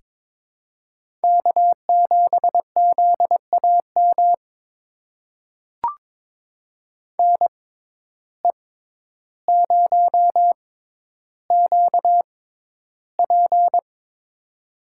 Morse code, heard as a keyed tone, about 22 wpm, spaced as for 11 wpm, 700 Hz. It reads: K7ZAM NE0QP
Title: USA Call Signs - 5 Characters - 6x Character Spacing - 22wpm